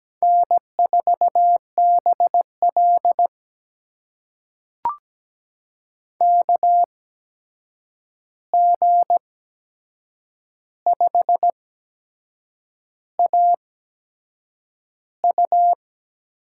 Morse code, heard as a keyed tone, about 17 wpm, spaced as for 7 wpm, 700 Hz. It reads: N4BL KG5AU
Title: USA Call Signs - 8x Character Spacing - 17wpm